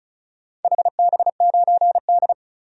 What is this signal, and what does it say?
Morse code, heard as a keyed tone, about 36 wpm, 700 Hz. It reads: HB9D
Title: DXCC-100 Call Signs - 4 Characters - 35wpm